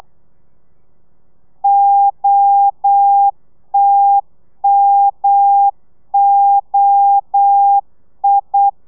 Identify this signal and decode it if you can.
Morse code, heard as a keyed tone, about 8 wpm, 800 Hz. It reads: OTMOI
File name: telegrafia / i2tfj / Lez1 35